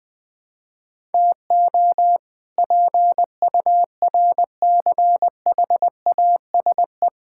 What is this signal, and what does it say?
Morse code, heard as a keyed tone, about 21 wpm, 700 Hz. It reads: TO PURCHASE
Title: Sentences from Top 1000 Words - Review Part 5 - 20wpm